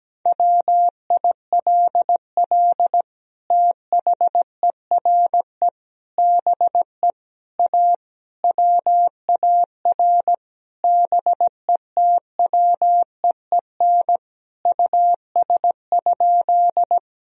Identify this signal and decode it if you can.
Morse code, heard as a keyed tone, about 17 wpm, 700 Hz. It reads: WILL THERE BE A WAR BETWEEN US?
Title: Sentences from Top 1000 Words - Review Part 2 - 17wpm